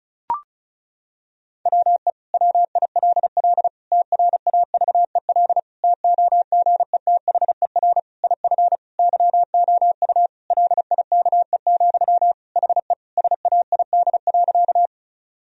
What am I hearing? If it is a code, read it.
Morse code, heard as a keyed tone, about 36 wpm, 700 Hz. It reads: WE WILL TRAVEL TOGETHER IF YOU LIKE, HE SAID.
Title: Sentences from Top 1000 Words - Review Part 2 - 35wpm